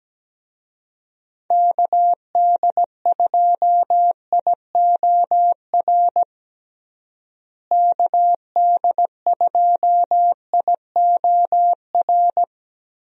Morse code, heard as a keyed tone, about 17 wpm, 700 Hz. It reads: KD2IOR KD2IOR
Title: USA Call Signs - 6 Characters - Repeated 2x - 17wpm